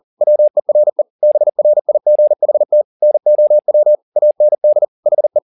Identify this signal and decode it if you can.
Morse code, heard as a keyed tone, about 40 wpm, 600 Hz. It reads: WERE BRIGHT NOW AND HE
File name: the old man and the sea 2 40wpm